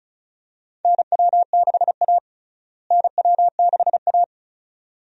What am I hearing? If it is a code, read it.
Morse code, heard as a keyed tone, about 36 wpm, 700 Hz. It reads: NW6A NW6A
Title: USA Call Signs - 4 Characters - Repeated 2x - 35wpm